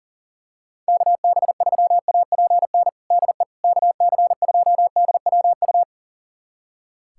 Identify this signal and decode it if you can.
Morse code, heard as a keyed tone, about 37 wpm, 700 Hz. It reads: KB3APN DE KC2DWU
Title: USA Callsign DE Callsign - 6 Characters - 40wpm